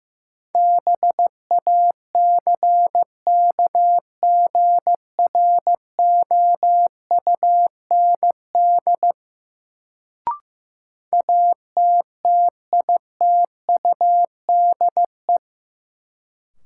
Morse code, heard as a keyed tone, about 15 wpm, 700 Hz. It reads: BACKGROUND ATTITUDE
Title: Sentences from Top 1300 Words - New Word Drill Alt - 15wpm